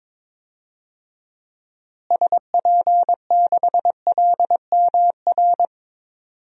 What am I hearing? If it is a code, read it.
Morse code, heard as a keyed tone, about 22 wpm, 700 Hz. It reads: SP6LMR
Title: DXCC-100 Call Signs - 6 Characters - Repeated 2x - 22wpm